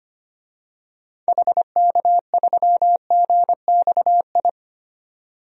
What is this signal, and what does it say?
Morse code, heard as a keyed tone, about 25 wpm, 700 Hz. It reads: HK3GXI
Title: DXCC-100 Call Signs - 6 Characters - 25wpm